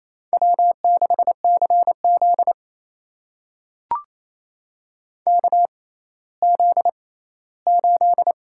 Morse code, heard as a keyed tone, about 28 wpm, spaced as for 16 wpm, 700 Hz. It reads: W6CZ KZ8
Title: USA Call Signs - 6x Character Spacing - 28wpm